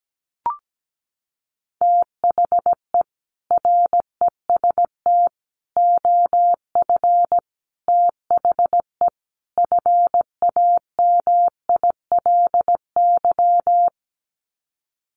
Morse code, heard as a keyed tone, about 17 wpm, 700 Hz. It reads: THE REST OF THE FAMILY
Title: Sets of 5 Words Encore2 17wpm